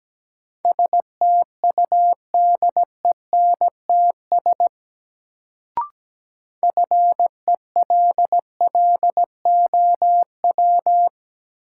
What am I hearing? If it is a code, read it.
Morse code, heard as a keyed tone, about 17 wpm, 700 Hz. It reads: STUDENTS FELLOW
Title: Sets of 5 Words Encore - New Word Drill - 17wpm